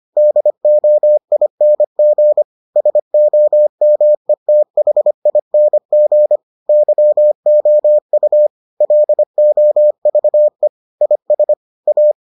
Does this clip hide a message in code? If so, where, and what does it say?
Yes: Morse code from 0.2 to 12.1 s, heard as a keyed tone, about 25 wpm, 600 Hz. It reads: DOING SOMETHING YOU LOVE IS A